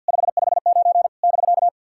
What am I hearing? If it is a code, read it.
Morse code, heard as a keyed tone, about 53 wpm, 690 Hz. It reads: 559 <BK>
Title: SOTA QSO Practice 50wpm